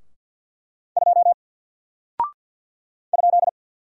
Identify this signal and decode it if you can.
Morse code, heard as a keyed tone, about 48 wpm, 700 Hz. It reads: 2 ?
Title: Single Character (Letter:Number:Punctuation:Prosign) Alt 50wpm